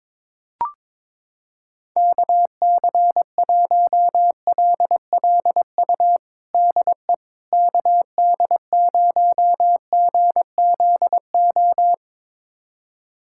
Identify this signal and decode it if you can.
Morse code, heard as a keyed tone, about 22 wpm, 700 Hz. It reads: KC1LLU DE KD0GZO
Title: USA Callsign DE Callsign 22wpm